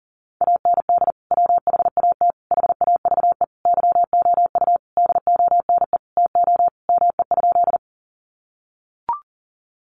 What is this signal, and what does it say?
Morse code, heard as a keyed tone, about 40 wpm, 700 Hz. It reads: AND WHAT HAVE YOU DONE TO ME?
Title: Sentences from Top 500 Words 40wpm